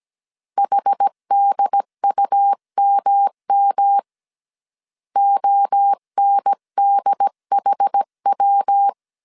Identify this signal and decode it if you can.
Morse code, heard as a keyed tone, about 17 wpm, 800 Hz. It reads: HDUMM ONDHW
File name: CW / DET / 1 3 1500C S A 16wpm 2141